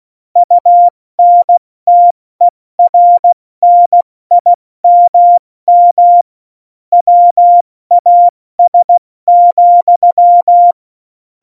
Morse code, heard as a keyed tone, about 16 wpm, spaced as for 14 wpm, 700 Hz. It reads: UNTERNIMM WAS,
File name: Morsekurs / Witze / Witz 04. Juni 2019, 70 F80